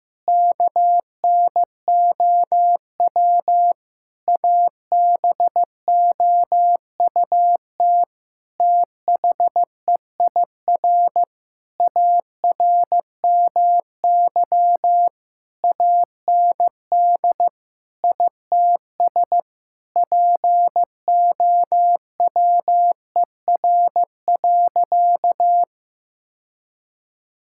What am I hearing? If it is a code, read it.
Morse code, heard as a keyed tone, about 15 wpm, 700 Hz. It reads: KNOW ABOUT THEIR ARMY AND ITS POWER.
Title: Sentences from Top 1000 Words - Review Part 5 - 15wpm